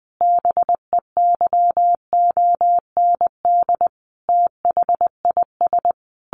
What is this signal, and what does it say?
Morse code, heard as a keyed tone, about 20 wpm, 700 Hz. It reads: BEYOND THIS
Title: A Sound Of Thunder 20wpm